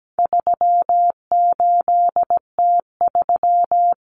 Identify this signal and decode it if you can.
Morse code, heard as a keyed tone, about 17 wpm, 700 Hz. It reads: 38T3
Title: Common Cut Numbers - 4-Digit 17wpm